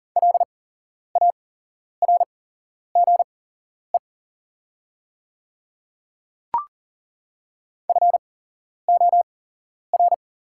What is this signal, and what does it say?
Morse code, heard as a keyed tone, about 40 wpm, spaced as for 11 wpm, 700 Hz. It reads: LARGE FOR
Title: 8x Character Spacing - Top 300 Words - 40wpm